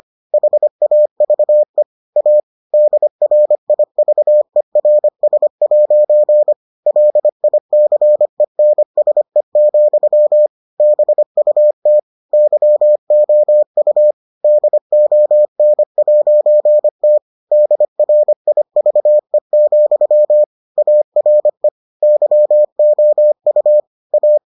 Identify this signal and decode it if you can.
Morse code, heard as a keyed tone, about 25 wpm, 600 Hz. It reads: HAVE A DRIVERS' LICENSE, BUT YOU DON'T DRIVE, ARE YOU A